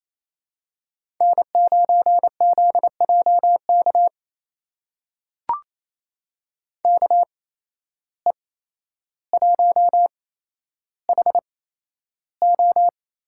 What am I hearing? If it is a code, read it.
Morse code, heard as a keyed tone, about 28 wpm, spaced as for 12 wpm, 700 Hz. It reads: N9ZJK KE1HO
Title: USA Call Signs - 5 Characters - 8x Character Spacing - 28wpm